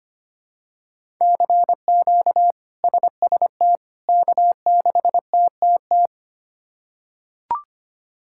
Morse code, heard as a keyed tone, about 25 wpm, 700 Hz. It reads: CQ SST K6TTT
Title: K1USN SST QSO Practice - 25wpm